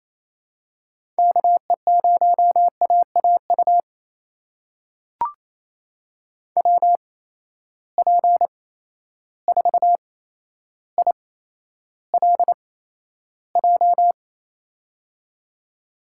Morse code, heard as a keyed tone, about 28 wpm, spaced as for 12 wpm, 700 Hz. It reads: KE0AAU WP4ILJ
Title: USA Call Signs - 6 Characters - 8x Character Spacing - 28wpm